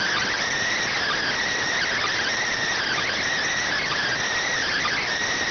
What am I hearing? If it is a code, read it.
Emergency Alert System (SAME) tones, heard as harsh buzzing bursts. It is heard from 1.0 to 5.1 s.